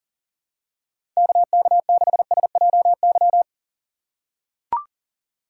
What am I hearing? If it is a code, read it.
Morse code, heard as a keyed tone, about 40 wpm, 700 Hz. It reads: KK6SJY
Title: USA Call Signs - 6 Characters - 3x Character Spacing - 40wpm